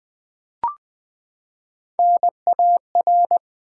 Morse code, heard as a keyed tone, about 20 wpm, 700 Hz. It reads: NAR
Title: Build A Word - 3 Letters At A Time - 9-Letter Words - 20wpm